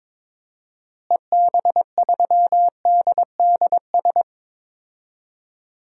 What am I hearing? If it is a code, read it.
Morse code, heard as a keyed tone, about 22 wpm, 700 Hz. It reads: EB3DDS
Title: DXCC-100 Call Signs - 6 Characters - Repeated 2x - 22wpm